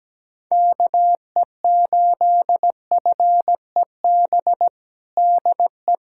Morse code, heard as a keyed tone, about 17 wpm, 700 Hz. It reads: KE8FEB DE
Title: USA Callsign DE Callsign - 6 Characters - 17wpm